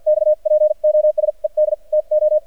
Morse code, heard as a keyed tone, about 54 wpm, spaced as for 42 wpm, 610 Hz. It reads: XPOSED TO
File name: archived / CRSnet